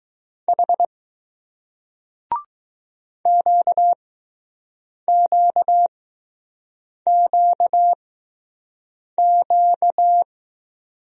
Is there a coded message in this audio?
Yes: Morse code from 0.5 to 10.2 s, heard as a keyed tone, about 20 wpm, spaced as for 6 wpm, 700 Hz. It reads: H QQQQ